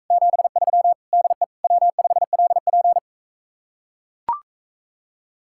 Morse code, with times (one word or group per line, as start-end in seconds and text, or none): 0.1-0.9 73
1.1-1.4 DE
1.6-3.0 W5LP